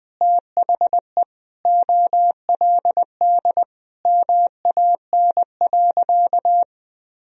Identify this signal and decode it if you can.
Morse code, heard as a keyed tone, about 20 wpm, 700 Hz. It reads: THE OLD MAN.
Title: Sentences from Top 500 Words and Abbreviations 20wpm